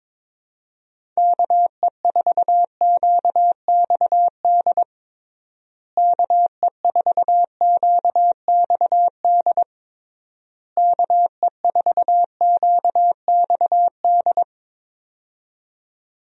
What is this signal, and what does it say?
Morse code, heard as a keyed tone, about 22 wpm, 700 Hz. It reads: KE4QXD KE4QXD KE4QXD
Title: USA Call Signs - 6 Characters - Repeated 3x - 22wpm